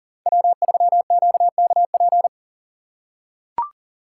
Morse code, heard as a keyed tone, about 40 wpm, 700 Hz. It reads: W3QKP
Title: USA Call Signs - 5 Characters - 2x Character Spacing - 40wpm